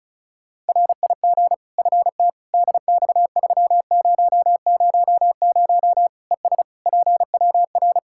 Morse code, heard as a keyed tone, about 34 wpm, 700 Hz. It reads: RIG FT DX3000 ES PWR
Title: Practice QSO - Exchange 2 - Info - 35wpm